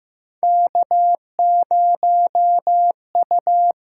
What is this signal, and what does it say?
Morse code, heard as a keyed tone, about 15 wpm, 700 Hz. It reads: K0U
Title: USA Call Signs - 6x Character Spacing - 15wpm